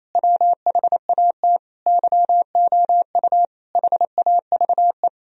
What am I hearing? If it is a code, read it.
Morse code, heard as a keyed tone, about 28 wpm, 700 Hz. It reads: WHAT YOU HAVE
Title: Sentences from Top 1000 Words - Review Part 3 - 28wpm